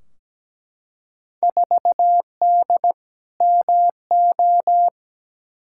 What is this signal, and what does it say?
Morse code, heard as a keyed tone, about 17 wpm, 700 Hz. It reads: 4D MO
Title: Field Day Exchanges Encore - 17wpm